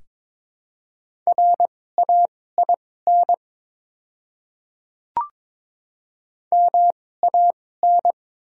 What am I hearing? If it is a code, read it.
Morse code, heard as a keyed tone, about 22 wpm, spaced as for 16 wpm, 700 Hz. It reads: RAIN MAN
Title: 2x Character Spacing - Top 300 Words - 22wpm